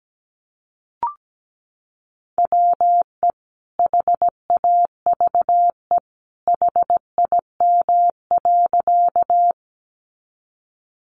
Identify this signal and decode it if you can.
Morse code, heard as a keyed tone, about 17 wpm, 700 Hz. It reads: WE HAVE HIM.
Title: Sentences from Top 100 Words 17wpm